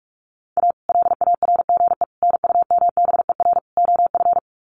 Morse code, heard as a keyed tone, about 44 wpm, 700 Hz. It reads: A LARGE NUMBER OF
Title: Sets of 4 Words 45wpm